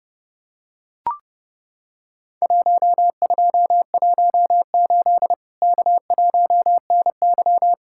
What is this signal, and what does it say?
Morse code, heard as a keyed tone, about 30 wpm, 700 Hz. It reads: 1218 K1NY
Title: Worked All Europe DX Contest - QTC Exchange - 30wpm